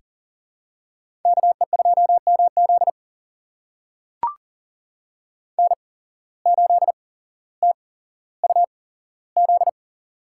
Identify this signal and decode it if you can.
Morse code, heard as a keyed tone, about 40 wpm, spaced as for 17 wpm, 700 Hz. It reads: KE2MZ N8TUZ
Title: USA Call Signs - 5 Characters - 8x Character Spacing - 40wpm